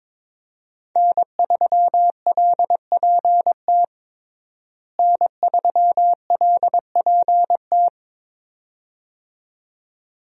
Morse code, heard as a keyed tone, about 22 wpm, 700 Hz. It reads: N3LPT N3LPT
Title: USA Call Signs - 5 Characters - Repeated 2x - 22wpm